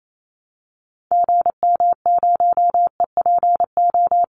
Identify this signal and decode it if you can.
Morse code, heard as a keyed tone, about 28 wpm, 700 Hz. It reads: GM0EPO
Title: Call Signs - 28wpm